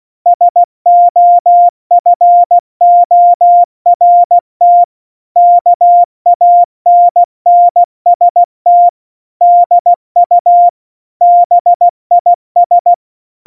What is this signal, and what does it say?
Morse code, heard as a keyed tone, about 16 wpm, 700 Hz. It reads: SOFORT KANNST DU BIS
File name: Morsekurs / Witze / Witz 24. September 2019, 70 F80